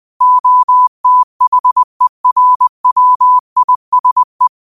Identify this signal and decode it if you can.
Morse code, heard as a keyed tone, about 20 wpm, 1000 Hz. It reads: OTHERWISE
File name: qotdfiles / 20wpm